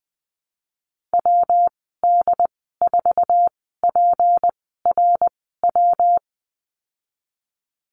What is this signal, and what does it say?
Morse code, heard as a keyed tone, about 19 wpm, spaced as for 14 wpm, 700 Hz. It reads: WD4PRW
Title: 161 Lesson 26g - Letter g - Call Signs 20wpm